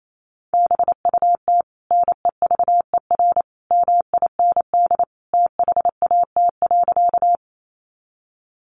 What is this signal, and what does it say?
Morse code, heard as a keyed tone, about 28 wpm, 700 Hz. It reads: BUT NEVER MIND THAT.
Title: Sentences from Top 500 Words 28wpm